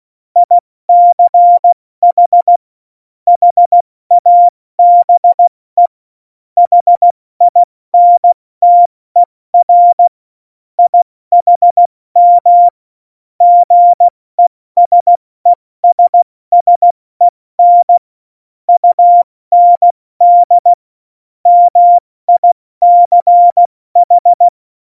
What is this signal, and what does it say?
Morse code, heard as a keyed tone, about 16 wpm, 700 Hz. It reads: ICH HABE HINTER IHM GESESSEN UND MICH
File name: Morsekurs / Witze / Witz 11 August 2020, 70 F80